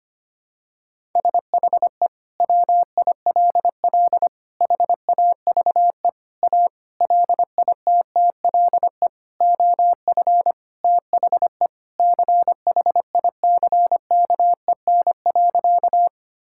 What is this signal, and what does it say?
Morse code, heard as a keyed tone, about 25 wpm, 700 Hz. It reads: SHE WILL HAVE A LITTLE OF THE CHICKEN.
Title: Sentences from Top 1000 Words - Review Part 5 - 25wpm